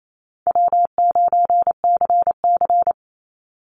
Morse code, heard as a keyed tone, about 28 wpm, 700 Hz. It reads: W9CC
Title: Easy Call Signs Repeated 3x - 28wpm